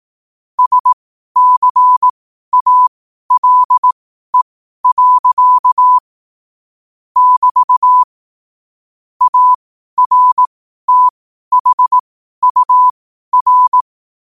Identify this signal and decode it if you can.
Morse code, heard as a keyed tone, about 18 wpm, spaced as for 12 wpm, 1000 Hz. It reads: SCALE. <BT> ARTHUR
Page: qotd-at-12-wpm